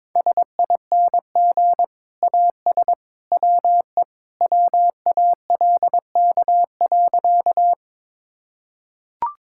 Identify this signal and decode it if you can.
Morse code, heard as a keyed tone, about 22 wpm, 700 Hz. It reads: SING AS WE WALK.
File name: practice / 22wpm / Sentences with 2-Word Phrases - 22wpm